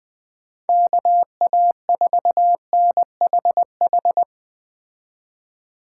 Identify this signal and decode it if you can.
Morse code, heard as a keyed tone, about 20 wpm, 700 Hz. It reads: KA4NHH
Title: Speed Racing - USA 2x3 Call Signs 17wpm